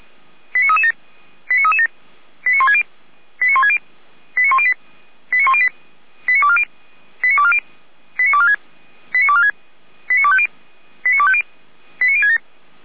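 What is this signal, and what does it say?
Five-tone selective call (ZVEI), heard as quick runs of five tones, tones of about 70 ms. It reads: 89337 89337 89160 89160 89108 89108 89240 89240 89246 89246 89250 89250 89067
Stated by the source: ZVEI five-tone selective call 89075 received with an SDR.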